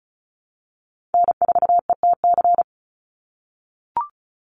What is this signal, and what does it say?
Morse code, heard as a keyed tone, about 34 wpm, 700 Hz. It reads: N4ETC
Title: Call Signs - 35wpm